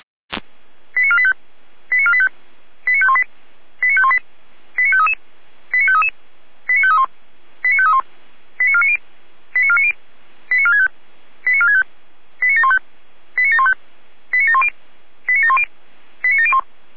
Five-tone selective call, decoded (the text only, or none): 89485 89485 89518 89518 89533 89533 89531 89531 89490 89490 89465 89465 89715 89715 89710 89710 89701